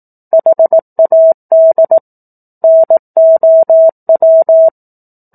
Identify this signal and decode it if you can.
Morse code, heard as a keyed tone, about 18 wpm, 650 Hz. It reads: HAD NOW
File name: 04-Chapter-04 18 WPM